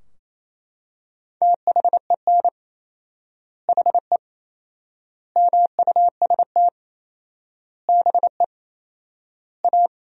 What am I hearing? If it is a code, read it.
Morse code, heard as a keyed tone, about 27 wpm, 700 Hz. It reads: THEN HE MUST BE A
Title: 4x Word-Spacing - Sentences from Top 500 Words - 28wpm